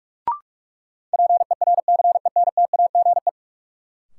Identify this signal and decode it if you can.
Morse code, heard as a keyed tone, about 44 wpm, 700 Hz. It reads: PERCENTAGE
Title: Sentences from Top 1300 Words - New Word Drill Alt - 45wpm